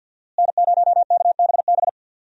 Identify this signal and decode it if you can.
Morse code, heard as a keyed tone, about 48 wpm, 700 Hz. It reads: N0KBB
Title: USA Call Signs - 5 Characters - Repeated 2x - 50wpm